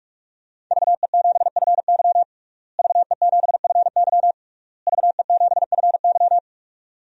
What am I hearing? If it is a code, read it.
Morse code, heard as a keyed tone, about 46 wpm, 700 Hz. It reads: VE7FY VE7FY VE7FY
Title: DXCC-100 Call Signs - 5 Characters - Repeated 3x - 45wpm